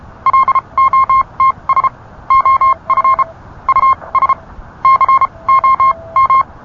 Morse code, heard as a keyed tone, about 32 wpm, spaced as for 25 wpm, 1060 Hz. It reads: LOTS OF US COM